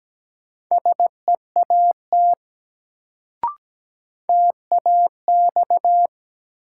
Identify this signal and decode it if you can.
Morse code, heard as a keyed tone, about 17 wpm, 700 Hz. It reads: SEAT TAX
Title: Sentences from Top 1500 Words - New Word Drill Alt - 17wpm